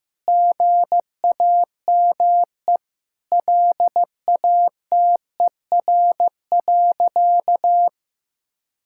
Morse code, heard as a keyed tone, about 15 wpm, 700 Hz. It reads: GAME LATER.
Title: Sentences from Top 1000 Words - Review Part 6 - 15wpm